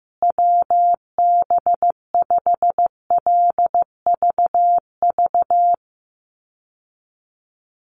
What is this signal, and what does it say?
Morse code, heard as a keyed tone, about 15 wpm, 700 Hz. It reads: WB5LVV
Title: USA Call Signs Repeated 2x 15wpm